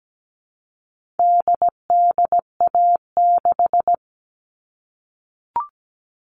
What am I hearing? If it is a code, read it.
Morse code, heard as a keyed tone, about 17 wpm, 700 Hz. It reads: DDA6
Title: All Cut Numbers - 4-Digit 17wpm